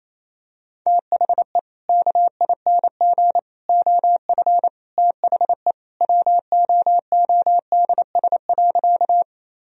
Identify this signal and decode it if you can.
Morse code, heard as a keyed tone, about 28 wpm, 700 Hz. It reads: THE KING OF THE WOODS.
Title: Sentences from Top 1000 Words - Review Part 2 - 28wpm